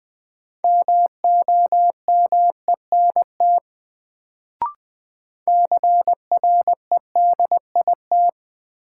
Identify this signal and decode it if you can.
Morse code, heard as a keyed tone, about 20 wpm, 700 Hz. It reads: MOMENT CREDIT
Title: Sentences from Top 1000 Words - New Word Drill - 20wpm